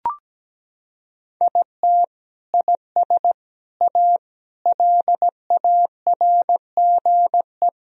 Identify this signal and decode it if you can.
Morse code, heard as a keyed tone, about 17 wpm, 700 Hz. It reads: IT IS A LARGE
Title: Sentences from Top 1000 Words - Review Part 4 - 17wpm